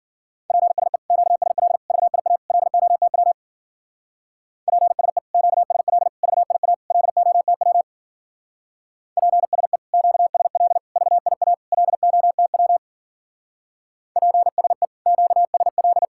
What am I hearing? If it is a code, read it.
Morse code, heard as a keyed tone, about 50 wpm, 700 Hz. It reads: PSE QSL VIA LOTW PSE QSL VIA LOTW PSE QSL VIA LOTW PSE QSL
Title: Speed-Racing Common QSO Phrases - 40wpm